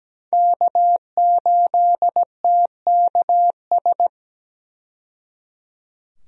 Morse code, heard as a keyed tone, about 17 wpm, 700 Hz. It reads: K8TKS
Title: USA Call Signs - 5 Characters - 17wpm